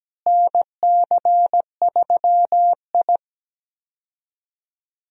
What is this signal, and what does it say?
Morse code, heard as a keyed tone, about 17 wpm, 700 Hz. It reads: NC3I
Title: USA Call Signs - 4 Characters - Repeated 2x - 17wpm